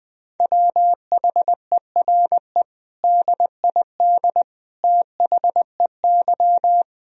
Morse code, heard as a keyed tone, about 20 wpm, 700 Hz. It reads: WHERE DID THEY
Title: Sentences from Top 400 Words and Abbreviations 20wpm